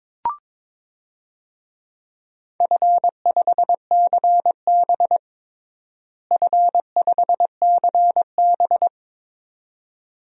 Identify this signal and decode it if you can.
Morse code, heard as a keyed tone, about 22 wpm, 700 Hz. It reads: F5CB F5CB
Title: DXCC-100 Call Signs - 4 Characters - Repeated 2x - 22wpm